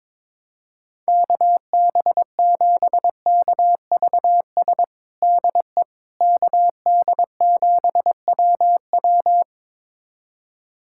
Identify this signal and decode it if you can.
Morse code, heard as a keyed tone, about 22 wpm, 700 Hz. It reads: KB7KVS DE KD7WW
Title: USA Callsign DE Callsign 22wpm